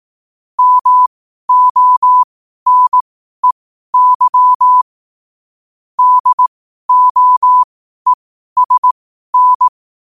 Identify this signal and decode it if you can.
Morse code, heard as a keyed tone, about 18 wpm, spaced as for 12 wpm, 1000 Hz. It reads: MONEY DOESN
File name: qotdfiles / 12wpm